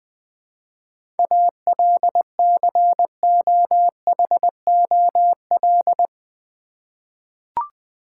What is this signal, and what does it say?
Morse code, heard as a keyed tone, about 21 wpm, 690 Hz. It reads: ALCOHOL
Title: 7 Letter Words - First 3 Letters Given - 20wpm